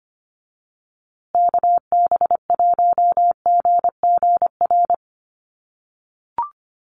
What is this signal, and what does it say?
Morse code, heard as a keyed tone, about 25 wpm, 700 Hz. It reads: KB1GGR
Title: USA Call Signs Repeated 3x 25wpm